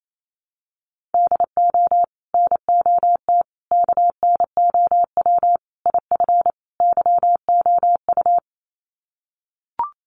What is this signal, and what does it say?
Morse code, heard as a keyed tone, about 28 wpm, 700 Hz. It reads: DO NOT KNOW IF YOU
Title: Sets of 5 Words 28wpm